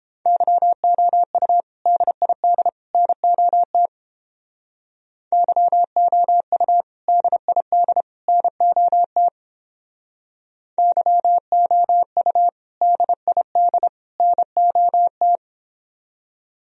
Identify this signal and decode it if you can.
Morse code, heard as a keyed tone, about 29 wpm, 700 Hz. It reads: YOU DID NOT YOU DID NOT YOU DID NOT